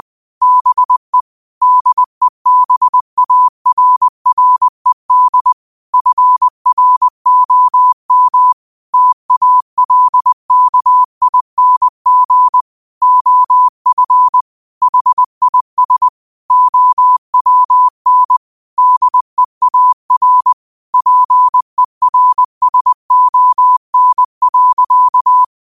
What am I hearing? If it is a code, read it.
Morse code, heard as a keyed tone, about 20 wpm, 1000 Hz. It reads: BE DEBARRED FROM TALKING OF HIS OWN DEAR PERSON.
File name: 20wpm